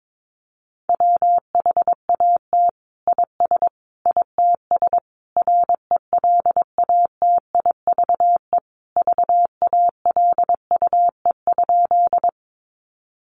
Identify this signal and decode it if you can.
Morse code, heard as a keyed tone, about 22 wpm, 700 Hz. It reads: WHAT IS ITS RELATIVE VALUE?
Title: Sentences from Top 1400 Words 22wpm